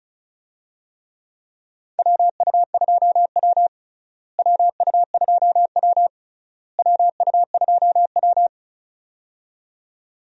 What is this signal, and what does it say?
Morse code, heard as a keyed tone, about 36 wpm, 700 Hz. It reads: WU2W WU2W WU2W
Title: USA Call Signs - 4 Characters - Repeated 3x - 35wpm